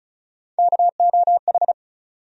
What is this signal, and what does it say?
Morse code, heard as a keyed tone, about 36 wpm, 700 Hz. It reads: KOH
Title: USA Call Sign Postfixes - 3 Letters 35wpm